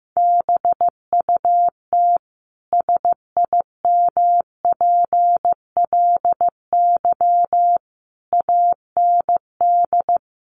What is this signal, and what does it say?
Morse code, heard as a keyed tone, about 15 wpm, 700 Hz. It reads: BUT SIMPLY AND
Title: Sentences from Top 1300 Words 15wpm